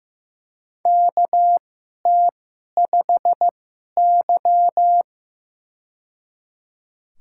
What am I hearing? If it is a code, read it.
Morse code, heard as a keyed tone, about 15 wpm, spaced as for 11 wpm, 700 Hz. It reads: KT5Y
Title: USA Call Signs - 4 Characters - 2x Character Spacing - 15wpm